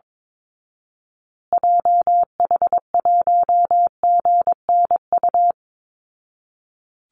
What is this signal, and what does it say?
Morse code, heard as a keyed tone, about 22 wpm, 690 Hz. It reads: JH1GNU
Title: Call Signs Repeated 2x - 22wpm